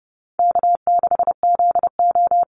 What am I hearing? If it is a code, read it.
Morse code, heard as a keyed tone, about 30 wpm, 700 Hz. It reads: K6ZO
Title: Easy Call Signs - 30wpm